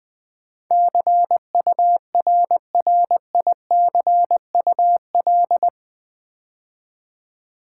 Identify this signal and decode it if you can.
Morse code, heard as a keyed tone, about 20 wpm, 700 Hz. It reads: CURRICUL
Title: Build A Word - 4 Letters At A Time - 10-Letter Words - 20wpm